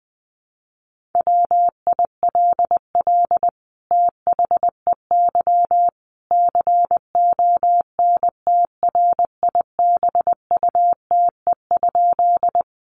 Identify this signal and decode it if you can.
Morse code, heard as a keyed tone, about 20 wpm, 700 Hz. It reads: WILL THEY CONTRIBUTE?
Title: Sentences from Top 1400 Words 20wpm